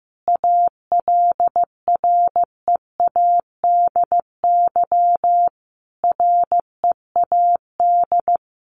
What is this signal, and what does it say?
Morse code, heard as a keyed tone, about 15 wpm, 700 Hz. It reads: ALREADY READ
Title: Sentences from Top 700 Words 15wpm